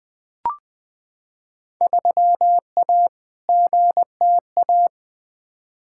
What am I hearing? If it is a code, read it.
Morse code, heard as a keyed tone, about 21 wpm, 700 Hz. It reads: 3A GTA
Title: Field Day Exchanges - 20wpm